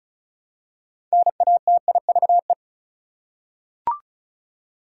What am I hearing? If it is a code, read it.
Morse code, heard as a keyed tone, about 34 wpm, 700 Hz. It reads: NATIVE
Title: Six Letter Words - 35wpm